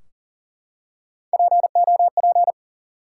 Morse code, heard as a keyed tone, about 42 wpm, 700 Hz. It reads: POP
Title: Three Letter Words - 40wpm